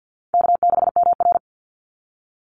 Morse code, heard as a keyed tone, about 50 wpm, 700 Hz. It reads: K6MR
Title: Call Signs - 50wpm